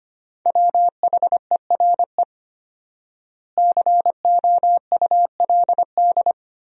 Morse code, heard as a keyed tone, about 25 wpm, 700 Hz. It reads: WHERE COULD
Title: 4x Word-Spacing - Sentences from Top 300 Words - 25wpm